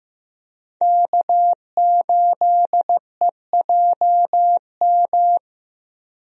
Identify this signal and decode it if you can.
Morse code, heard as a keyed tone, about 15 wpm, 700 Hz. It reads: K8EJM
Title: USA Call Signs - 5 Characters - Repeated 2x - 15wpm